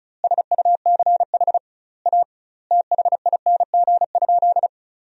Morse code, heard as a keyed tone, about 36 wpm, 700 Hz. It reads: SUCH A THING?
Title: 2x Word-Spacing - Sentences from Top 200 Words - 35wpm